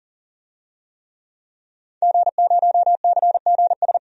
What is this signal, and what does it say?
Morse code, heard as a keyed tone, about 40 wpm, 700 Hz. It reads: G0CGS
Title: DXCC-100 Call Signs - 5 Characters - 40wpm